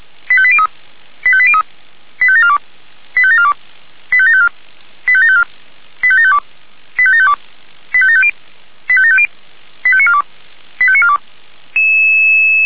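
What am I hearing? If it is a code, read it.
Five-tone selective call (ZVEI), heard as quick runs of five tones, tones of about 70 ms. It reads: 85803 85803 85742 85742 85754 85754 85752 85752 85750 85750 85942 85942
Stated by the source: ZVEI five-tone selective call received by SDR.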